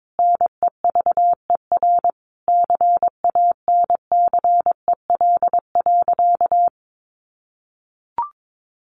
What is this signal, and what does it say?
Morse code, heard as a keyed tone, about 22 wpm, 700 Hz. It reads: NEVER CANCEL.
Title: Sentences from Top 1200 Words 22wpm